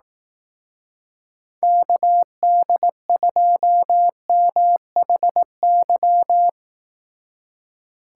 Morse code, heard as a keyed tone, about 19 wpm, 700 Hz. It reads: KD2MHY
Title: Speed Racing - USA 2x3 Call Signs 15wpm